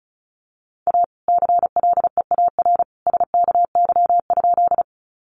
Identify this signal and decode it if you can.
Morse code, heard as a keyed tone, about 34 wpm, 700 Hz. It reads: A CLEAR SKY?
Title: Sentences from Top 1200 Words 35wpm